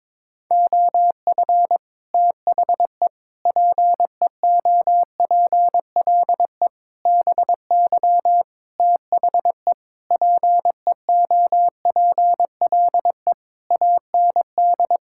Morse code, heard as a keyed tone, about 22 wpm, 700 Hz. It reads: OF THE PEOPLE BY THE PEOPLE AND
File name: Sets of 9 Words - 22wpm